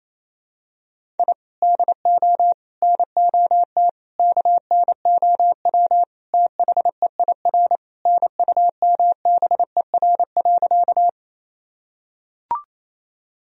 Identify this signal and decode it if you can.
Morse code, heard as a keyed tone, about 28 wpm, 700 Hz. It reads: I DO NOT KNOW THEIR NUMBER.
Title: Sentences from Top 1000 Words - Review Part 1 - 28wpm